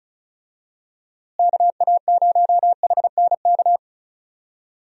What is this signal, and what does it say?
Morse code, heard as a keyed tone, about 36 wpm, 700 Hz. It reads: KA0HNK
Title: USA Call Signs - 6 Characters - Repeated 3x - 35wpm